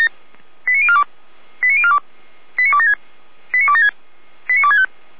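Five-tone selective call (ZVEI), heard as quick runs of five tones, tones of about 70 ms. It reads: 89042 89042 89276 89276 89265
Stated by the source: ZVEI five-tone call 89294 received by SDR.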